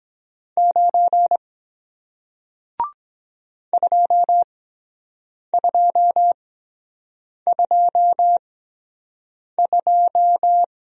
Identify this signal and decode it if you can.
Morse code, heard as a keyed tone, about 23 wpm, spaced as for 7 wpm, 700 Hz. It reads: 9 2222